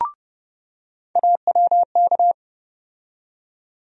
Morse code, heard as a keyed tone, about 31 wpm, 700 Hz. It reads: AWK
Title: USA Call Sign Postfixes - 3 Letters 30wpm